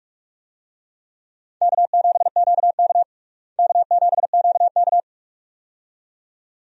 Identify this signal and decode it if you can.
Morse code, heard as a keyed tone, about 46 wpm, 700 Hz. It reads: K7QK K7QK
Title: USA Call Signs - 4 Characters - Repeated 2x - 45wpm